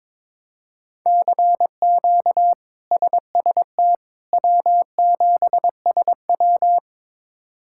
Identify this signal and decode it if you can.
Morse code, heard as a keyed tone, about 22 wpm, 700 Hz. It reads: CQ SST W7SW
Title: K1USN SST QSO Practice - 22wpm